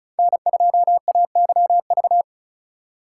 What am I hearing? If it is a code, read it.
Morse code, heard as a keyed tone, about 36 wpm, 700 Hz. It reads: N2AYV
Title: USA Call Signs - 5 Characters - 3x Character Spacing - 35wpm